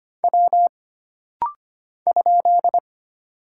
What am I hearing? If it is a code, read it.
Morse code, heard as a keyed tone, about 25 wpm, 700 Hz. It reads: W ?
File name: Single Character (Letter:Number:Punctuation:Prosign) - Rapid-Fire - 25wpm